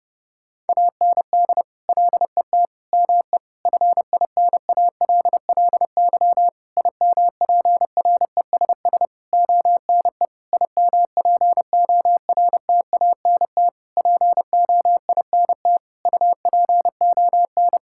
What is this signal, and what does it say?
Morse code, heard as a keyed tone, about 32 wpm, 700 Hz. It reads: AND LET ME FINALLY IMPRESS ONE IMPORTANT POINT UPON
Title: Sentences from Top 1600 Words 30wpm